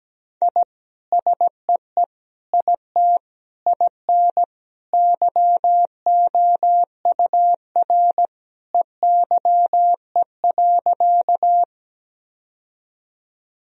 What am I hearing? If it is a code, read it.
Morse code, heard as a keyed tone, about 17 wpm, 700 Hz. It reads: I SEE IT IN YOUR EYE.
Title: Sentences from Top 1000 Words - Review Part 1 - 17wpm